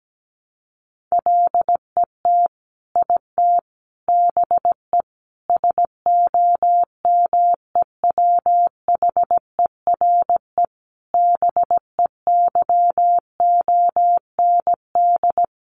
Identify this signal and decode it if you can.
Morse code, heard as a keyed tone, about 17 wpm, 700 Hz. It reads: LET IT BE SOMEWHERE BEYOND
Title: Sentences from Top 1400 Words 17wpm